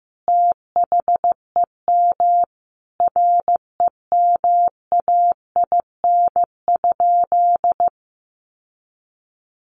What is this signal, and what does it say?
Morse code, heard as a keyed tone, about 15 wpm, 700 Hz. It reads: THEM REMAIN?
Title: Sentences from Top 1200 Words 15wpm